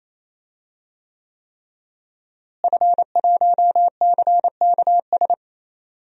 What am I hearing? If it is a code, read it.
Morse code, heard as a keyed tone, about 28 wpm, 700 Hz. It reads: F1CKS
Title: DXCC-100 Call Signs - 5 Characters - Repeated 3x - 28wpm